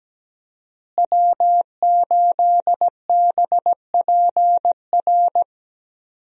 Morse code, heard as a keyed tone, about 17 wpm, 700 Hz. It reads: W8BPR
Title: USA Call Signs - 5 Characters - Repeated 3x - 17wpm